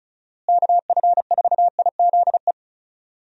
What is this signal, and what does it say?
Morse code, heard as a keyed tone, about 36 wpm, 700 Hz. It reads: KF4IZE
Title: USA Call Signs - 6 Characters - Repeated 2x - 35wpm